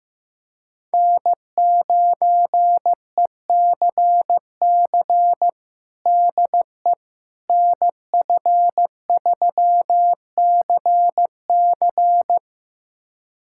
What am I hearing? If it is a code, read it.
Morse code, heard as a keyed tone, about 15 wpm, 700 Hz. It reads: N9ECC DE NF3CC
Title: USA Callsign DE Callsign 15wpm